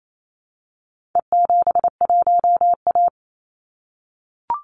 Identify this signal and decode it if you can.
Morse code, heard as a keyed tone, about 28 wpm, 700 Hz. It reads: E71A
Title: Call Signs - 28wpm